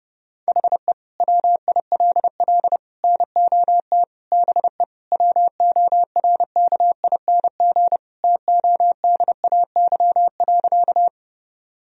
Morse code, heard as a keyed tone, about 32 wpm, 700 Hz. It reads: HE WILL NOT BE WORKING TODAY.
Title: Sentences from Top 1000 Words - Review Part 4 - 30wpm